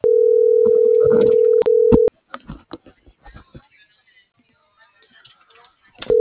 Ringback tone (North America), 440 + 480 Hz.